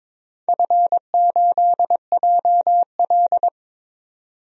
Morse code, heard as a keyed tone, about 23 wpm, 700 Hz. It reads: F8JL
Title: DXCC-100 Call Signs - 4 Characters - 22wpm